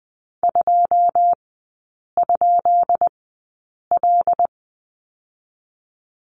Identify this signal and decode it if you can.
Morse code, heard as a keyed tone, about 20 wpm, spaced as for 8 wpm, 700 Hz. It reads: 2?L
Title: 152 Lesson 25e - Letter p - Three Characters 20wpm